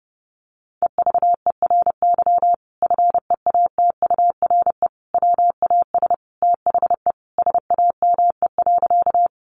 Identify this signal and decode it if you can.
Morse code, heard as a keyed tone, about 28 wpm, 700 Hz. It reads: EVERY FEATURE WAS THE SAME.
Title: Sentences from Top 800 Words 30wpm